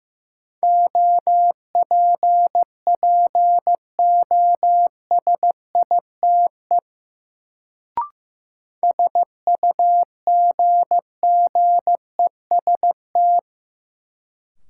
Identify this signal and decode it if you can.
Morse code, heard as a keyed tone, about 15 wpm, 700 Hz. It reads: OPPOSITE SUGGEST
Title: Sentences from Top 1300 Words - New Word Drill Alt - 15wpm